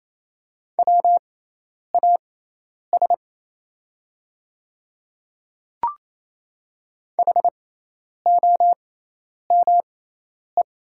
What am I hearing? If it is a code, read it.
Morse code, heard as a keyed tone, about 27 wpm, spaced as for 10 wpm, 700 Hz. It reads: WAS HOME
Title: 6x Character Spacing - Top 300 Words - 28wpm